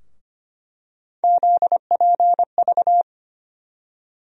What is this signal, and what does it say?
Morse code, heard as a keyed tone, about 25 wpm, 700 Hz. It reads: ZPV
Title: USA Call Sign Postfixes - 3 Letters 25wpm